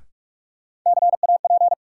Morse code, heard as a keyed tone, about 46 wpm, 700 Hz. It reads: CAP
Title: Sentences from Top 1600 Words - New Word Drill Alt - 45wpm